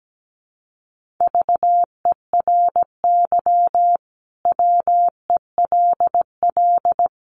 Morse code, heard as a keyed tone, about 17 wpm, 700 Hz. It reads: VERY WELL
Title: Sentences from Top 200 Words 17wpm